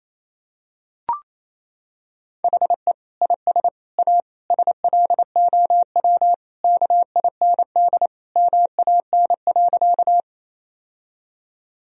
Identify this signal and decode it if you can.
Morse code, heard as a keyed tone, about 28 wpm, 700 Hz. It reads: HE IS A SLOW KIND MAN.
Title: Sentences from Top 1000 Words - Review Part 2 - 28wpm